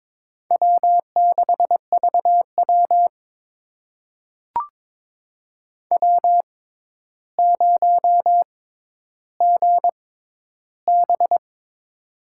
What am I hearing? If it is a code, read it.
Morse code, heard as a keyed tone, about 22 wpm, spaced as for 11 wpm, 700 Hz. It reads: W6VW W0GB
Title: USA Call Signs - 4 Characters - 6x Character Spacing - 22wpm